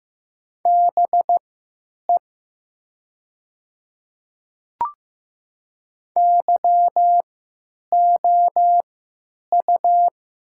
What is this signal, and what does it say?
Morse code, heard as a keyed tone, about 15 wpm, spaced as for 9 wpm, 700 Hz. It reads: BE YOU
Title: 3x Character Spacing - Top 200 Words - 15wpm